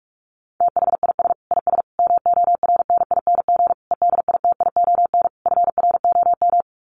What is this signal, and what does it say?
Morse code, heard as a keyed tone, about 44 wpm, 700 Hz. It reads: THIS IS MORNING EDITION FROM
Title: Sets of 5 Words Encore2 45wpm